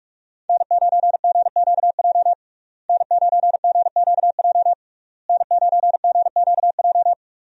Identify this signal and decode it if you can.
Morse code, heard as a keyed tone, about 47 wpm, 700 Hz. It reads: N9GQJ N9GQJ N9GQJ
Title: USA Call Signs - 5 Characters - Repeated 3x - 45wpm